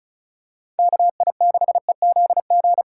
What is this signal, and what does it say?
Morse code, heard as a keyed tone, about 36 wpm, 700 Hz. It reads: KI6EZG
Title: USA Call Signs - 6 Characters - 3x Character Spacing - 35wpm